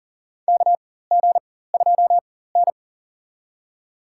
Morse code, heard as a keyed tone, about 40 wpm, spaced as for 19 wpm, 700 Hz. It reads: KG2N
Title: USA Call Signs - 4 Characters - 4x Character Spacing - 40wpm